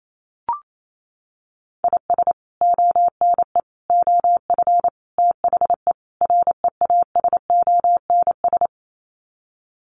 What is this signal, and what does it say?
Morse code, heard as a keyed tone, about 28 wpm, 700 Hz. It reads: IS ONE OF THE REASONS
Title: Sets of 5 Words Encore2 28wpm